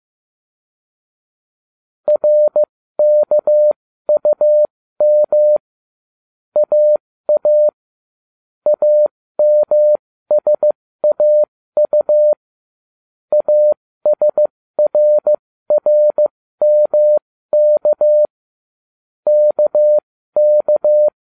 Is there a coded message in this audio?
Yes: Morse code from 2.1 to 21.1 s, heard as a keyed tone, about 15 wpm, spaced as for 13 wpm, 600 Hz. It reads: RKUM AA AMSAU ASRRMK KK